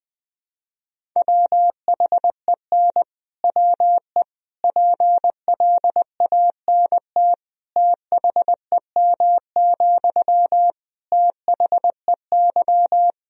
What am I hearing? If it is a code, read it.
Morse code, heard as a keyed tone, about 20 wpm, 700 Hz. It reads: WHEN WE PLANT THEM, THEY
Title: Sentences from Top 1000 Words - Review Part 1 - 20wpm